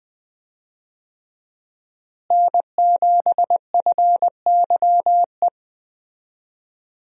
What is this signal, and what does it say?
Morse code, heard as a keyed tone, about 20 wpm, 700 Hz. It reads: N7FYE
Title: USA Call Signs - 5 Characters - Repeated 2x - 20wpm